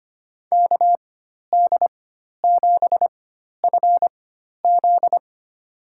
Morse code, heard as a keyed tone, about 25 wpm, spaced as for 12 wpm, 700 Hz. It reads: KD7FZ
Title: USA Call Signs - 5 Characters - 4x Character Spacing - 25wpm